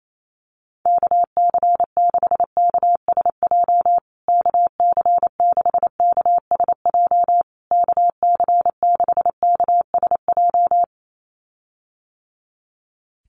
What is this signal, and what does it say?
Morse code, heard as a keyed tone, about 28 wpm, 700 Hz. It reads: KC6KSJ KC6KSJ KC6KSJ
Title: USA Call Signs Repeated 3x 28wpm